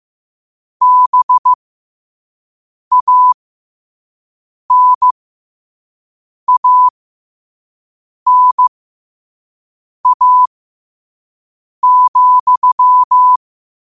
Morse code, heard as a keyed tone, about 15 wpm, spaced as for 5 wpm, 1000 Hz. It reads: BANANA,
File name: qotdfiles / 5wpm